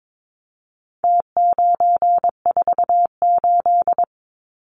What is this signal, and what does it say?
Morse code, heard as a keyed tone, about 22 wpm, 700 Hz. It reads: T948
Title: Numbers - Contest Style - 4-Digit 22wpm